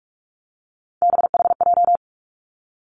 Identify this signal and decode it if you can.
Morse code, heard as a keyed tone, about 46 wpm, 700 Hz. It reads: BHJ
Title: Three Challenging Letter-Number Combinations 45wpm